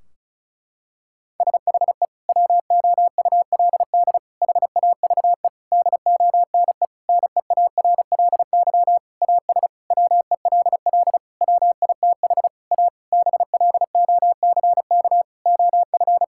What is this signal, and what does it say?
Morse code, heard as a keyed tone, about 36 wpm, 700 Hz. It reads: SHE WOULD HAVE DONE NEARLY AS WELL WITH A BLOCK OF
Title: Sentences from Top 1600 Words 35wpm